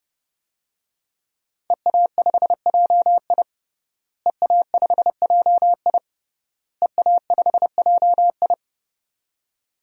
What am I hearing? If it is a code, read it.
Morse code, heard as a keyed tone, about 30 wpm, 700 Hz. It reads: EA5JI EA5JI EA5JI
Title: DXCC-100 Call Signs - 5 Characters - Repeated 3x - 30wpm